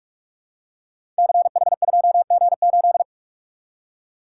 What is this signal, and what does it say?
Morse code, heard as a keyed tone, about 44 wpm, 700 Hz. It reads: CH2G8
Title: Code Groups - 5 Letter:Numbers - 45wpm